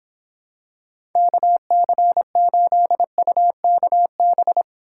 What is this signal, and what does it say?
Morse code, heard as a keyed tone, about 26 wpm, 700 Hz. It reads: KC8UKB